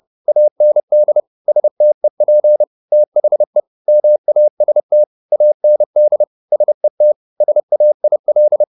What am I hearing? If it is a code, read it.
Morse code, heard as a keyed tone, about 30 wpm, 600 Hz. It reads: AND STEP THE MAST AND SET SAIL